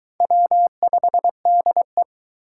Morse code, heard as a keyed tone, about 23 wpm, 700 Hz. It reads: W5DE